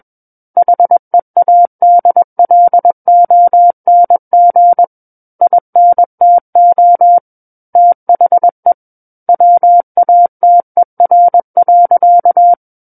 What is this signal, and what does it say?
Morse code, heard as a keyed tone, about 21 wpm, 700 Hz. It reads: HEADLONG INTO THE WATER.
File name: Book One / 12-Chapter 12 - 21 WPM